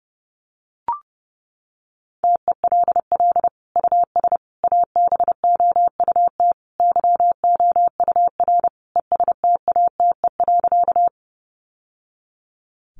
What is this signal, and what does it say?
Morse code, heard as a keyed tone, about 30 wpm, 700 Hz. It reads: TELL US ABOUT YOUR ESTATE.
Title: Sentences from Top 1200 Words 30wpm